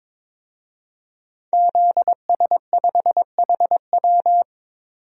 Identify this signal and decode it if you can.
Morse code, heard as a keyed tone, about 22 wpm, 700 Hz. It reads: ZS5HW
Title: DXCC-100 Call Signs - 5 Characters - Repeated 3x - 22wpm